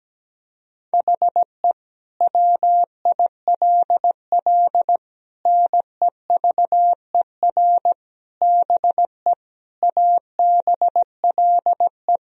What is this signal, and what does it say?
Morse code, heard as a keyed tone, about 17 wpm, 700 Hz. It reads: HE WILL NEVER BE ABLE
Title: Sentences from Top 1000 Words - Review Part 2 - 17wpm